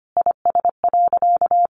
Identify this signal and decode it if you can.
Morse code, heard as a keyed tone, about 25 wpm, 700 Hz. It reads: IS.
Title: Sentences from Top 700 Words 25wpm